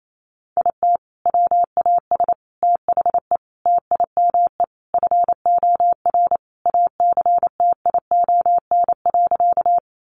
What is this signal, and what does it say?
Morse code, heard as a keyed tone, about 28 wpm, 700 Hz. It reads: IT WAS THE TIME FOR ACTION.
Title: Sentences from Top 700 Words 28wpm